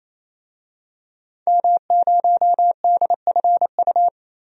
Morse code, heard as a keyed tone, about 27 wpm, 700 Hz. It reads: M0DFU
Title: DXCC-100 Call Signs - 5 Characters - 28wpm